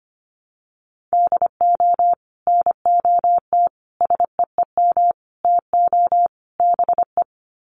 Morse code, heard as a keyed tone, about 25 wpm, 700 Hz. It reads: DO NOT SEEM TO BE
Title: Sets of 5 Words Encore2 25wpm